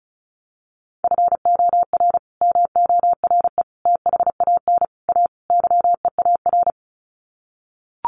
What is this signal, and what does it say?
Morse code, heard as a keyed tone, about 34 wpm, 700 Hz. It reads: FOR MORE THAN A YEAR
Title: Sets of 5 Words 35wpm